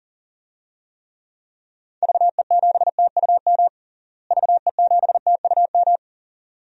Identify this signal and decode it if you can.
Morse code, heard as a keyed tone, about 40 wpm, 700 Hz. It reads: VE7TUM VE7TUM
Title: DXCC-100 Call Signs - 6 Characters - Repeated 2x - 40wpm